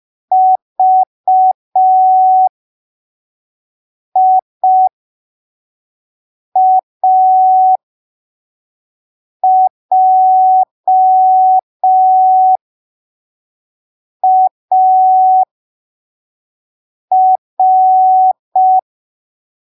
Morse code, heard as a keyed tone, about 5 wpm, spaced as for 3 wpm, 750 Hz. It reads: VIAJAR
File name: arquivos / texto / texto 21 05ppm